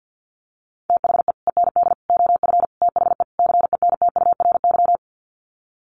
Morse code, heard as a keyed tone, about 50 wpm, 700 Hz. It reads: THE END OF THE CENTURY
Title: Sets of 5 Words Encore2 50wpm